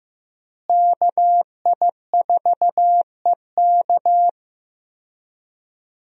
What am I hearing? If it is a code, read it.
Morse code, heard as a keyed tone, about 15 wpm, 700 Hz. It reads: KI4EK
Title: USA Call Signs - 5 Characters - Repeated 3x - 15wpm